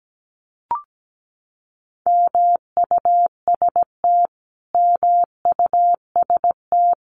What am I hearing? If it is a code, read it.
Morse code, heard as a keyed tone, about 17 wpm, 700 Hz. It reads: MUST MUST
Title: Top 300 Words Repeated 17wpm